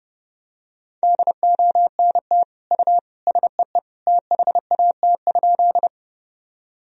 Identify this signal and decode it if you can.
Morse code, heard as a keyed tone, about 32 wpm, 700 Hz. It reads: DONT U SEE THAT?
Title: Sentences from Top 300 Words and Abbreviations 30wpm